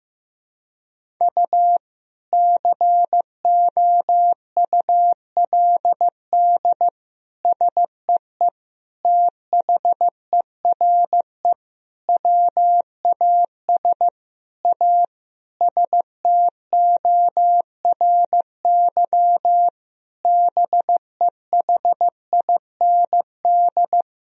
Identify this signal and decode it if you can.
Morse code, heard as a keyed tone, about 15 wpm, 700 Hz. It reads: U COULD SEE THERE WAS A STORY BEHIND
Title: Sentences from Top 500 Words and Abbreviations 15wpm